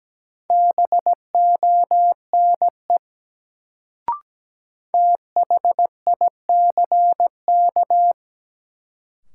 Morse code, heard as a keyed tone, about 17 wpm, 700 Hz. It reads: BONE THICK
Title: Sentences from Top 1600 Words - New Word Drill Alt - 17wpm